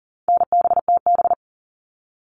Morse code, heard as a keyed tone, about 38 wpm, 700 Hz. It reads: NBTB
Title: All Cut Numbers - 4-Digit 40wpm